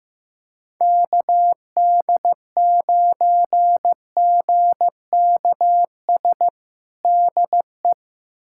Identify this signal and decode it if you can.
Morse code, heard as a keyed tone, about 15 wpm, 700 Hz. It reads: KD9GKS DE
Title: USA Callsign DE Callsign - 6 Characters - 15wpm